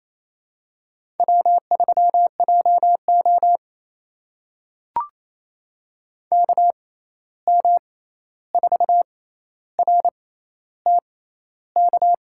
Morse code, heard as a keyed tone, about 28 wpm, spaced as for 13 wpm, 700 Hz. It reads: W3JO KM4RTK
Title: USA Call Signs - 6x Character Spacing - 28wpm